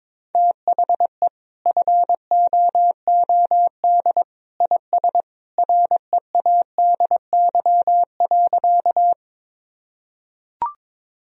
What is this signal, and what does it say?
Morse code, heard as a keyed tone, about 22 wpm, 700 Hz. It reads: THE FOOD IS READY.
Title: Sentences from Top 1000 Words - Review Part 1 - 22wpm